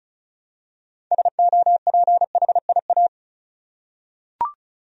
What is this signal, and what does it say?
Morse code, heard as a keyed tone, about 34 wpm, 700 Hz. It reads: SOPHIA
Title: USA Common Names - 35wpm